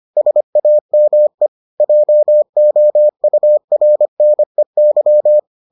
Morse code, heard as a keyed tone, about 25 wpm, 600 Hz. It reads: SAME JOURNEY